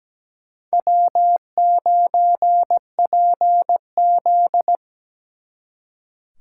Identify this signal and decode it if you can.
Morse code, heard as a keyed tone, about 20 wpm, 700 Hz. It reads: W9PZ
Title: Speed Racing - USA 1x2 Call Signs 17wpm